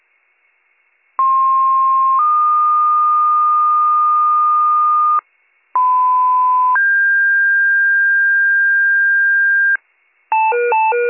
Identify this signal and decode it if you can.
Two-tone page, heard as a one-second tone, then a three-second tone, about 1060 then 1220 Hz, about 1010 then 1600 Hz, with noise. It is heard from 1.2 to 9.8 s.